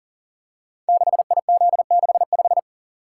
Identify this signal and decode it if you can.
Morse code, heard as a keyed tone, about 38 wpm, 690 Hz. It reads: 6IZ65
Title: Code Groups - 5 Letter:Numbers - 40wpm